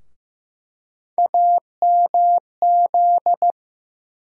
Morse code, heard as a keyed tone, about 15 wpm, 700 Hz. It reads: AMZ
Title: USA Call Sign Postfixes - 3 Letters 15wpm